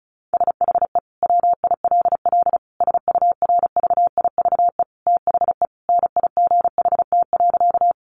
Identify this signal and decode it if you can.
Morse code, heard as a keyed tone, about 34 wpm, 700 Hz. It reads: SHE WILL SURVIVE THE NIGHT.
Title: Sentences from Top 1300 Words 35wpm